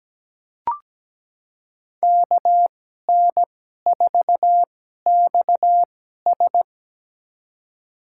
Morse code, heard as a keyed tone, about 17 wpm, spaced as for 12 wpm, 700 Hz. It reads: KN4XS
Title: USA Call Signs - 2x Character Spacing - 17wpm